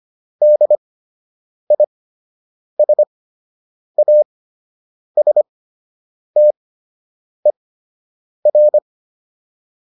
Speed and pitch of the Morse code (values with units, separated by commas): 25 wpm, spaced as for 8 wpm, 600 Hz